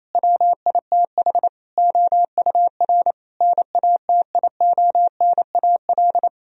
Morse code, heard as a keyed tone, about 28 wpm, 700 Hz. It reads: WITH OUR NATIONAL
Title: Sentences from Top 1000 Words - Review Part 4 - 28wpm